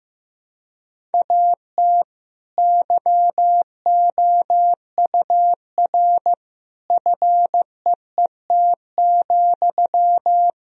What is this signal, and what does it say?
Morse code, heard as a keyed tone, about 15 wpm, 700 Hz. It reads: AT YOUR FEET,
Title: Sentences from Top 1000 Words - Review Part 1 - 15wpm